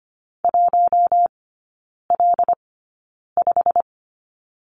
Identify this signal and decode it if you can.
Morse code, heard as a keyed tone, about 25 wpm, spaced as for 9 wpm, 690 Hz. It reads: 1L5
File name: Lesson 15e - Number 5 - Three Characters 25wpm